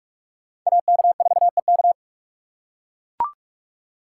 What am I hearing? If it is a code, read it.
Morse code, heard as a keyed tone, about 44 wpm, 700 Hz. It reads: AK4EK
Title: USA Call Signs - 5 Characters - 8x Character Spacing - 45wpm